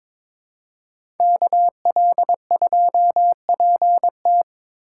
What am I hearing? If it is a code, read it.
Morse code, heard as a keyed tone, about 22 wpm, 700 Hz. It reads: KL2PT
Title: USA Call Signs - 5 Characters - 2x Character Spacing - 22wpm